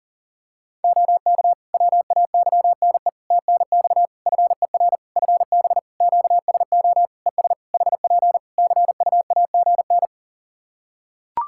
Morse code, heard as a keyed tone, about 37 wpm, 700 Hz. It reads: OK WAYNE TNX FER FB QSO ES HP CUAGN
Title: Practice QSO - Exchange 4 - Ending - 40wpm